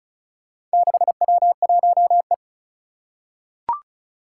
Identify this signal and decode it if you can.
Morse code, heard as a keyed tone, about 36 wpm, 700 Hz. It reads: 6W1E
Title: Four Challenging Letter-Number Combinations 35wpm